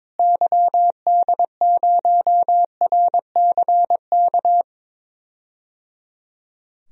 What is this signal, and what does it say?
Morse code, heard as a keyed tone, about 22 wpm, 700 Hz. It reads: YD0RCK
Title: DXCC-100 Call Signs - 6 Characters - Repeated 3x - 22wpm